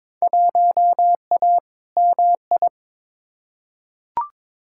Morse code, heard as a keyed tone, about 22 wpm, 700 Hz. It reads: 1A MI
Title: Field Day Exchanges - 22wpm